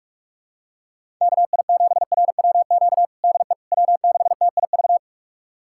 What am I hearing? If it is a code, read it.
Morse code, heard as a keyed tone, about 46 wpm, 700 Hz. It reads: KI7RWQ DE W6TIV
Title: USA Callsign DE Callsign 45wpm